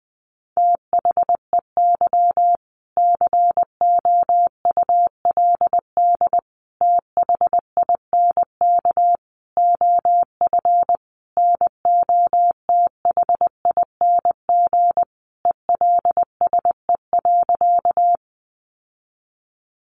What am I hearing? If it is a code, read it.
Morse code, heard as a keyed tone, about 20 wpm, 700 Hz. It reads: THEY COULD THINK OF NOTHING ELSE.
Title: Sentences from Top 800 Words 20wpm